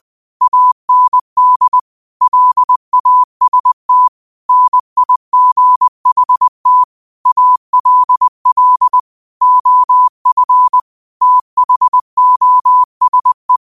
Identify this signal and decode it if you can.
Morse code, heard as a keyed tone, about 20 wpm, 1000 Hz. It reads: AND LAST NIGHT ALL OF THOSE
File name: qotdfiles / 20wpm